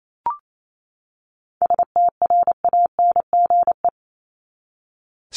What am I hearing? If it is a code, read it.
Morse code, heard as a keyed tone, about 28 wpm, 700 Hz. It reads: STRANGE
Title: Top 1501-1750 Words 28wpm